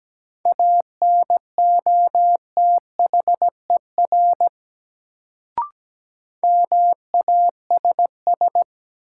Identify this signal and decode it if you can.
Morse code, heard as a keyed tone, about 17 wpm, 700 Hz. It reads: ANOTHER MASS
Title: Sets of 5 Words Encore - New Word Drill - 17wpm